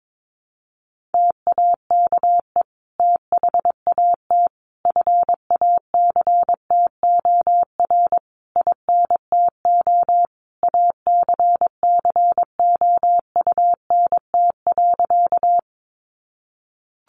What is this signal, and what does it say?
Morse code, heard as a keyed tone, about 22 wpm, 700 Hz. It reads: TAKE THAT FACTOR INTO ACCOUNT.
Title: Sentences from Top 1300 Words 22wpm